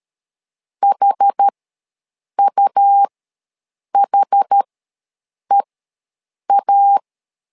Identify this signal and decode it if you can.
Morse code, heard as a keyed tone, about 13 wpm, spaced as for 7 wpm, 800 Hz. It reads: HUHEA
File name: LEZIONI / 2 UHDN / 2 UH ws 12wpm 1730